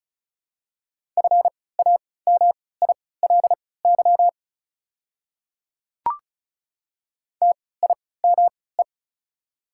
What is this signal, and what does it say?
Morse code, heard as a keyed tone, about 36 wpm, spaced as for 20 wpm, 700 Hz. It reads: FAMILY TIME
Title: 3x Character Spacing - Top 500 Words - 35wpm